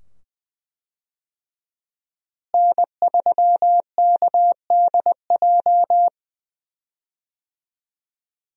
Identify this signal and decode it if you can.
Morse code, heard as a keyed tone, about 20 wpm, 700 Hz. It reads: N3KDJ
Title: USA Call Signs - 5 Characters - Repeated 2x - 20wpm